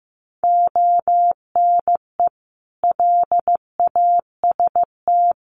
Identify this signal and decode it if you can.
Morse code, heard as a keyed tone, about 15 wpm, 700 Hz. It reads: ONE LAST
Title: Sentences from Top 1200 Words 15wpm